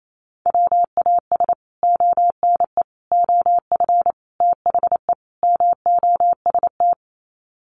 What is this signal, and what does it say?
Morse code, heard as a keyed tone, about 28 wpm, 700 Hz. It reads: WAS ONE OF THE MOST
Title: Sets of 5 Words 28wpm